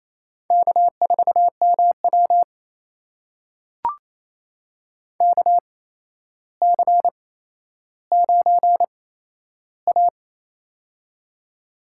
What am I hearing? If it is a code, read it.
Morse code, heard as a keyed tone, about 28 wpm, spaced as for 12 wpm, 700 Hz. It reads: K4MW KC9A
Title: USA Call Signs - 4 Characters - 8x Character Spacing - 28wpm